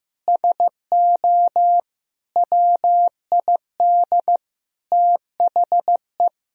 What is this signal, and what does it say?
Morse code, heard as a keyed tone, about 15 wpm, 700 Hz. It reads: SO WID THE
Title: Sentences from Top 200 Words and Abbreviations 15wpm